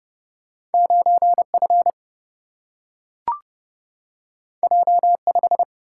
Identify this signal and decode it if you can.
Morse code, heard as a keyed tone, about 31 wpm, 700 Hz. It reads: 9 F J 5
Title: Single-Letter Single-Number 30wpm